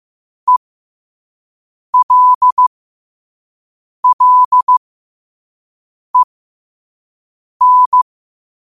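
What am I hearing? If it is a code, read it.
Morse code, heard as a keyed tone, about 15 wpm, spaced as for 5 wpm, 1000 Hz. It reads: ELLEN
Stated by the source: Morse code quote of the day sent at 5 WPM.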